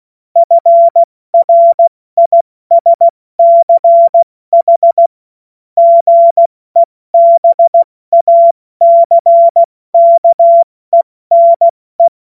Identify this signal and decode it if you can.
Morse code, heard as a keyed tone, about 16 wpm, 680 Hz. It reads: FRISCH GEBACKENE
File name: Morsekurs / Witze / Witz 06. Dezember 2022 70 F80